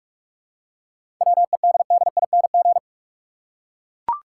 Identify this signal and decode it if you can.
Morse code, heard as a keyed tone, about 43 wpm, 700 Hz. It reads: WEDDING
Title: Seven Letter Words - 45wpm